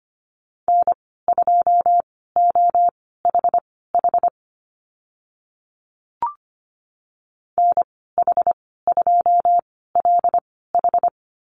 Morse code, heard as a keyed tone, about 25 wpm, spaced as for 16 wpm, 690 Hz. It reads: N2OHH NH2LH
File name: Lesson 15g - Number 5 - Call Signs 25wpm